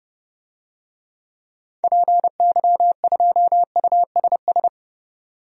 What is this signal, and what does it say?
Morse code, heard as a keyed tone, about 30 wpm, 690 Hz. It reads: PY2USS
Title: DXCC-100 Call Signs - 6 Characters - 30wpm